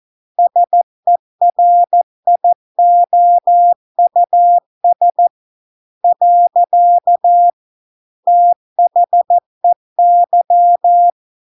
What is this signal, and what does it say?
Morse code, heard as a keyed tone, about 14 wpm, 700 Hz. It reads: SERIOUS . THEY